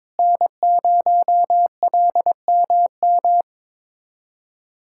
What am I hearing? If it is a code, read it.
Morse code, heard as a keyed tone, about 22 wpm, 700 Hz. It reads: N0LMM
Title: USA Callsign DE Callsign 22wpm